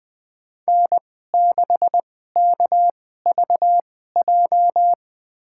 Morse code, heard as a keyed tone, about 20 wpm, spaced as for 15 wpm, 700 Hz. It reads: N6KVJ
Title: USA Call Signs - 5 Characters - 2x Character Spacing - 20wpm